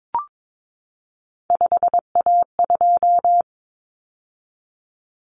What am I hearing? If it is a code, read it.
Morse code, heard as a keyed tone, about 22 wpm, 700 Hz. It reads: 5A2
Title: Common Cut Numbers - 3-Digit 22wpm